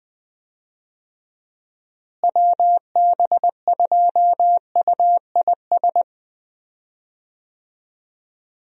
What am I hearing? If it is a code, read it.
Morse code, heard as a keyed tone, about 20 wpm, 700 Hz. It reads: WB2UIS
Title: USA Call Signs - 6 Characters - Repeated 2x - 20wpm